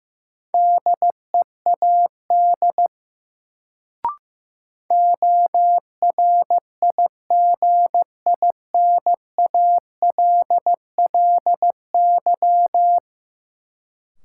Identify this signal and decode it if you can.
Morse code, heard as a keyed tone, about 15 wpm, 700 Hz. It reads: DEAD ORIGINALLY
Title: Sentences from Top 1500 Words - New Word Drill Alt - 15wpm